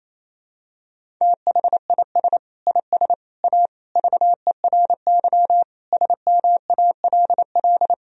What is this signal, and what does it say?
Morse code, heard as a keyed tone, about 28 wpm, 700 Hz. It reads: THIS IS A VERY SMALL
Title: Sentences from Top 1000 Words - Review Part 1 - 28wpm